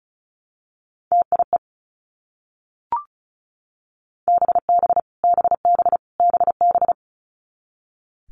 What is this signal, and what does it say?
Morse code, heard as a keyed tone, about 36 wpm, 700 Hz. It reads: TIE BB BB BB
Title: Multi-letter Phonemes Repeated 3x 35wpm